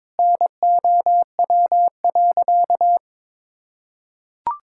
Morse code, heard as a keyed tone, about 21 wpm, 700 Hz. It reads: NOW.
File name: practice / 22wpm / Easy Sentences with Hints - 22wpm